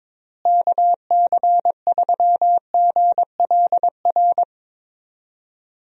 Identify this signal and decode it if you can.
Morse code, heard as a keyed tone, about 22 wpm, 700 Hz. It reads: KC3GLR
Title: USA Call Signs - 6 Characters - Repeated 3x - 22wpm